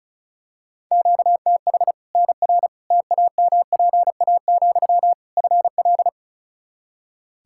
Morse code, heard as a keyed tone, about 36 wpm, 700 Hz. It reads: QTH NR TAMPA, FL
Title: Practice QSO - Exchange 1 - Beginning - 35wpm